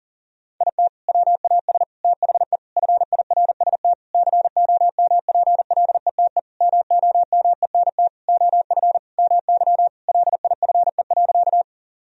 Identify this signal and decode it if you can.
Morse code, heard as a keyed tone, about 37 wpm, 700 Hz. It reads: IT WAS THE FIRST COMPLETE MOMENT OF MY LIFE.
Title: Sentences from Top 1000 Words - Review Part 2 - 40wpm